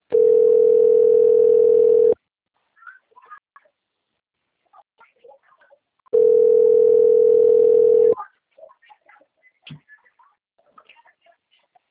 Ringback tone (North America), 440 + 480 Hz.